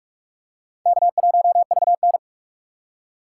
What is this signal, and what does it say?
Morse code, heard as a keyed tone, about 46 wpm, 700 Hz. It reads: K1VN
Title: USA Call Signs - 2x Character Spacing - 45wpm